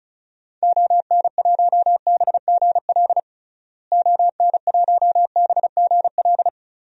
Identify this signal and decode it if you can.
Morse code, heard as a keyed tone, about 36 wpm, 700 Hz. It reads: ON1BGL ON1BGL
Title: DXCC-100 Call Signs - 6 Characters - Repeated 2x - 35wpm